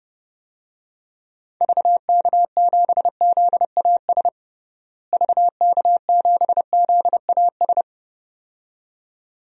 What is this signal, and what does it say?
Morse code, heard as a keyed tone, about 32 wpm, 700 Hz. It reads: VK7ZAS VK7ZAS
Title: DXCC-100 Call Signs - 6 Characters - Repeated 2x - 30wpm